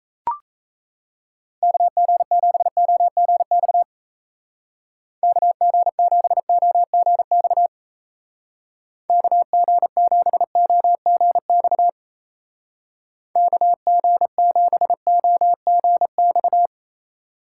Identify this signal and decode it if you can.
Morse code, heard as a keyed tone, about 37 wpm, 700 Hz. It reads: KG7OGX KG7OGX KG7OGX KG7OGX